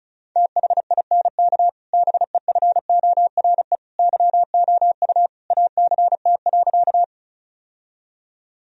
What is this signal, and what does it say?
Morse code, heard as a keyed tone, about 36 wpm, 700 Hz. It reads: THINK BEFORE YOU ACT.
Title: Sentences from Top 1000 Words - Review Part 1 - 35wpm